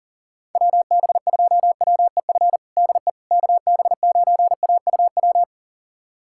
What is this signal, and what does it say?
Morse code, heard as a keyed tone, about 40 wpm, 700 Hz. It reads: WB2WEF DE KB9AUW
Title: USA Callsign DE Callsign 40wpm